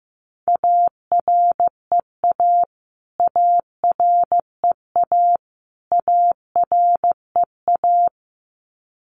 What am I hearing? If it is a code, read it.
Morse code, heard as a keyed tone, about 15 wpm, 700 Hz. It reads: AREA AREA AREA
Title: Top 500 Words Repeated 15wpm